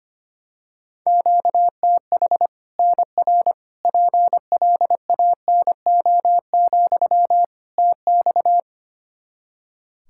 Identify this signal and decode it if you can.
Morse code, heard as a keyed tone, about 25 wpm, 700 Hz. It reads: QTH NR PLANO, TX
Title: Practice QSO - Exchange 1 - Beginning - 25wpm